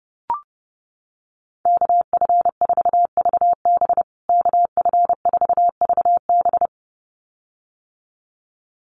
Morse code, heard as a keyed tone, about 30 wpm, 700 Hz. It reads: KF4VB KF4VB
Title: USA Call Signs Repeated 2x 30wpm